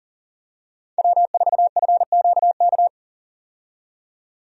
Morse code, heard as a keyed tone, about 40 wpm, 700 Hz. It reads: W4FQK
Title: USA Call Signs - 5 Characters - 40wpm